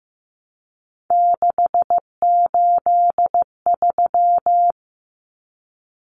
Morse code, heard as a keyed tone, about 14 wpm, 700 Hz. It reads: 683
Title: Numbers - 3-Digit 15wpm